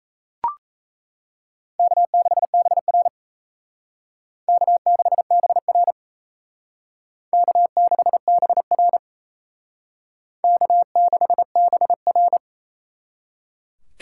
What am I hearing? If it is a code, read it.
Morse code, heard as a keyed tone, about 35 wpm, 700 Hz. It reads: K6BR K6BR K6BR K6BR